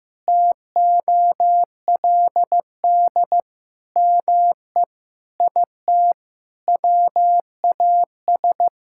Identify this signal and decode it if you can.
Morse code, heard as a keyed tone, about 15 wpm, 700 Hz. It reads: TOLD ME IT WAS
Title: Sentences from Top 1600 Words 15wpm